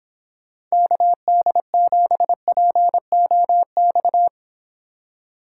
Morse code, heard as a keyed tone, about 24 wpm, 700 Hz. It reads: KD7POX